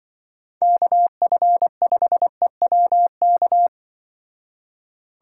Morse code, heard as a keyed tone, about 26 wpm, 700 Hz. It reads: KF5EWK